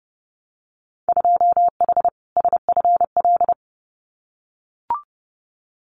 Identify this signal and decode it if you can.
Morse code, heard as a keyed tone, about 32 wpm, 700 Hz. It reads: 2H SFL
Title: Winter Field Day Exchanges Encore 30wpm